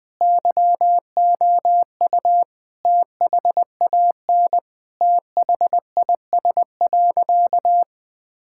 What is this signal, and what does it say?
Morse code, heard as a keyed tone, about 20 wpm, 700 Hz. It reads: YOU THAN THIS.
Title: Sentences from Top 1000 Words - Review Part 6 - 20wpm